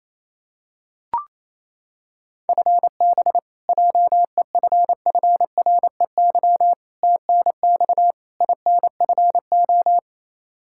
Morse code, heard as a keyed tone, about 28 wpm, 700 Hz. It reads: FB JEFFREY TNX INFO
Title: K1USN SST QSO Practice 28wpm